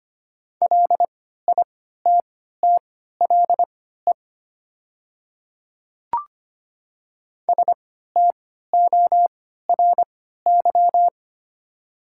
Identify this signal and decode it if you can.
Morse code, heard as a keyed tone, about 25 wpm, spaced as for 14 wpm, 700 Hz. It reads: LITTLE STORY
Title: 3x Character Spacing - Top 300 Words - 25wpm